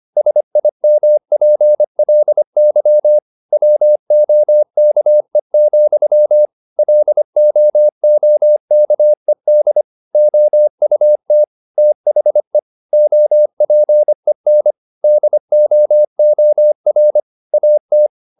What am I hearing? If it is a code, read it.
Morse code, heard as a keyed tone, about 25 wpm, 600 Hz. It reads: SIMPLY WOKE, LOOKED OUT THE OPEN DOOR AT
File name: the old man and the sea part 1 25wpm